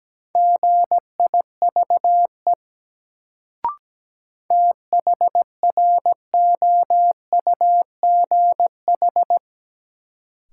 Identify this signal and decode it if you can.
Morse code, heard as a keyed tone, about 17 wpm, 700 Hz. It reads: GIVE THROUGH
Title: Sets of 2 Words - New Word Drill - 17wpm